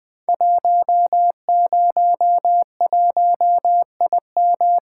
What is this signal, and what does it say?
Morse code, heard as a keyed tone, about 21 wpm, 700 Hz. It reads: 101IM
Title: Code Groups - 5 Letter:Numbers - 20wpm